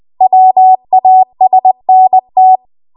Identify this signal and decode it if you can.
Morse code, heard as a keyed tone, about 20 wpm, 750 Hz. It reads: WASNT